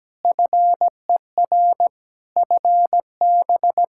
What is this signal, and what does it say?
Morse code, heard as a keyed tone, about 17 wpm, 700 Hz. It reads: FER FB
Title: Practice QSO - Exchange 4 - Ending - 17wpm